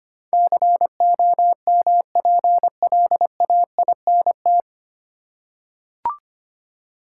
Morse code, heard as a keyed tone, about 25 wpm, 700 Hz. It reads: COMPLAINT
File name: Nine Letter Words - 25wpm